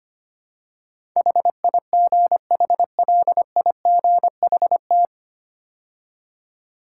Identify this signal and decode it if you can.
Morse code, heard as a keyed tone, about 25 wpm, 690 Hz. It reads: HIGHLIGHT
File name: practice / 25wpm / 9 Letter Words - First 3 Letters Given - 25wpm